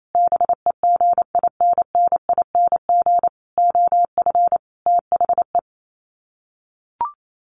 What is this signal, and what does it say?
Morse code, heard as a keyed tone, about 28 wpm, 700 Hz. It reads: BEGINNING OF THE
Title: Sets of 5 Words Encore2 28wpm